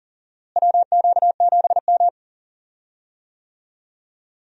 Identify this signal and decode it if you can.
Morse code, heard as a keyed tone, about 40 wpm, 700 Hz. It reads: WQ7M
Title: USA Call Signs - 4 Characters - Repeated 3x - 40wpm